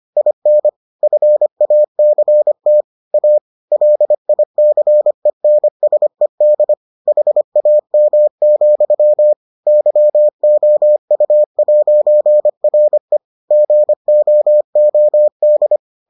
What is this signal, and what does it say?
Morse code, heard as a keyed tone, about 25 wpm, 600 Hz. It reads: IN FACT A LICENSED HAM, YOU'RE GOOD